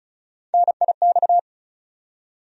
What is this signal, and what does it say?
Morse code, heard as a keyed tone, about 34 wpm, 700 Hz. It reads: NIX
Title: USA Call Sign Postfixes - 3 Letters 35wpm